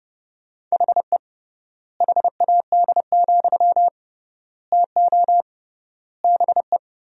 Morse code, heard as a keyed tone, about 30 wpm, 700 Hz. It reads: HE HAD, TO BE
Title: 3x Word-Spacing - Sentences from Top 500 Words - 30wpm